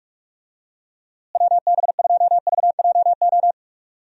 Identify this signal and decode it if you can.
Morse code, heard as a keyed tone, about 44 wpm, 700 Hz. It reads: WB2VJO
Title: USA Call Signs - 6 Characters - Repeated 2x - 45wpm